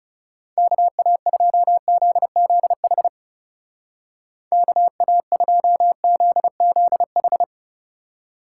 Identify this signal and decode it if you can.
Morse code, heard as a keyed tone, about 38 wpm, 700 Hz. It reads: KA2ZZH KA2ZZH